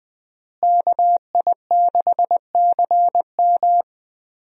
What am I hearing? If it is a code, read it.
Morse code, heard as a keyed tone, about 20 wpm, 700 Hz. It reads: KI6CM
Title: USA Call Signs - 5 Characters - Repeated 3x - 20wpm